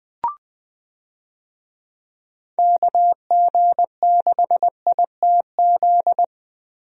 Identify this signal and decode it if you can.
Morse code, heard as a keyed tone, about 20 wpm, 700 Hz. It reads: KG6ITZ
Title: USA Call Signs - 6 Characters - Repeated 2x - 20wpm